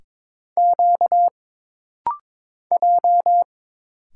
Morse code, heard as a keyed tone, about 22 wpm, 700 Hz. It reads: Q J
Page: This rapid-fire timing is geared to push your ICR (Instant Character Recognition) skills.